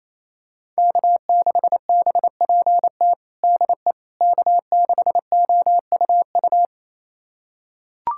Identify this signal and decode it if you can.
Morse code, heard as a keyed tone, about 28 wpm, 700 Hz. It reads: K6BPT DE K6OUU
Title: USA Callsign DE Callsign - 5 Characters - 28wpm